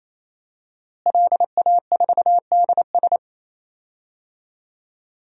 Morse code, heard as a keyed tone, about 28 wpm, 700 Hz. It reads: LA4DS
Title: DXCC-100 Call Signs - 5 Characters - Repeated 3x - 28wpm